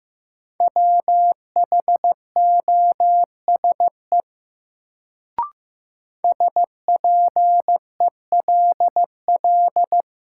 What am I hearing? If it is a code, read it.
Morse code, heard as a keyed tone, about 15 wpm, 700 Hz. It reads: WHOSE SPELL
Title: Sentences from Top 1200 Words - New Word Drill Alt - 15wpm